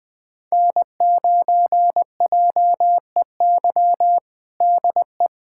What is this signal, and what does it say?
Morse code, heard as a keyed tone, about 20 wpm, 700 Hz. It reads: N9JEY DE
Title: USA Callsign DE Callsign - 5 Characters - 20wpm